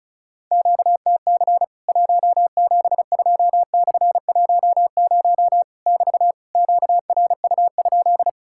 Morse code, heard as a keyed tone, about 35 wpm, 700 Hz. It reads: QTC 172/10 <BT> QRU?
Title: Worked All Europe DX Contest - QTC Exchange - 35wpm